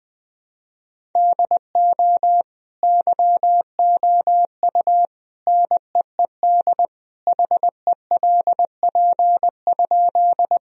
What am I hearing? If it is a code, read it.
Morse code, heard as a keyed tone, about 20 wpm, 700 Hz. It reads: DO YOU NEED HELP?
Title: Sentences from Top 1000 Words - Review Part 1 - 20wpm